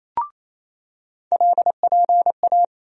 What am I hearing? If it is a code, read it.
Morse code, heard as a keyed tone, about 28 wpm, 700 Hz. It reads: LPA
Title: USA Call Sign Postfixes - 3 Letters 28wpm